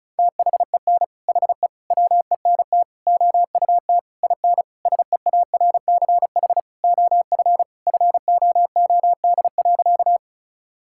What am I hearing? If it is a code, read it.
Morse code, heard as a keyed tone, about 36 wpm, 700 Hz. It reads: THEN HE WENT OUT IN SEARCH OF FOOD.
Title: Sentences from Top 1000 Words - Review Part 5 - 35wpm